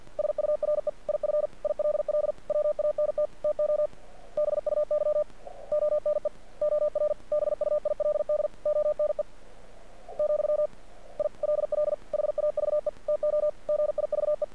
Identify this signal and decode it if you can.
Morse code, heard as a keyed tone, about 47 wpm, 600 Hz. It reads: SURE IF ILL WANT TO BUY ONE OR BUILD ONE , I LL HAVE TO GIVE
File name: archived / CRSnet